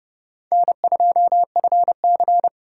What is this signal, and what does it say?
Morse code, heard as a keyed tone, about 36 wpm, 700 Hz. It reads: N2FC
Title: Speed Racing - USA 1x2 Call Signs 30wpm